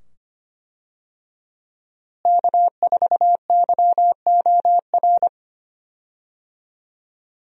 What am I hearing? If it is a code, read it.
Morse code, heard as a keyed tone, about 25 wpm, 700 Hz. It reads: K4YOR
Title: USA Call Signs - 5 Characters - Repeated 2x - 25wpm